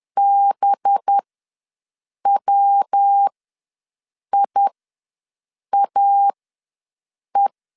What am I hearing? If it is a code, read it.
Morse code, heard as a keyed tone, about 10 wpm, spaced as for 6 wpm, 800 Hz. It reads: BWIAE
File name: LEZIONI / 6 PKX0 / 6 P ws 10wpm 3751